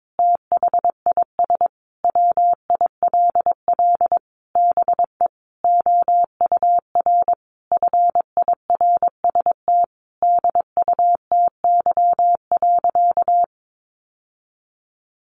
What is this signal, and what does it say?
Morse code, heard as a keyed tone, about 22 wpm, 700 Hz. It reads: THIS WILL BE OUR FIRST DUTY.
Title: Sentences from Top 1500 Words 22wpm